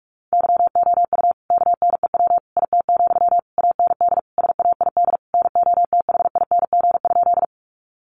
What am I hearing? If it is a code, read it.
Morse code, heard as a keyed tone, about 44 wpm, 700 Hz. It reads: YOU KNEW IT, AND SAID NOTHING?
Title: Sentences from Top 500 Words 45wpm